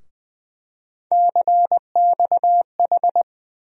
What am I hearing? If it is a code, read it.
Morse code, heard as a keyed tone, about 21 wpm, 700 Hz. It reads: CXH
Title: USA Call Sign Postfixes - 3 Letters 20wpm